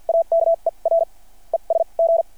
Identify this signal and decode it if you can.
Morse code, heard as a keyed tone, about 51 wpm, spaced as for 38 wpm, 660 Hz. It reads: AKER ES G
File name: archived / CRSnet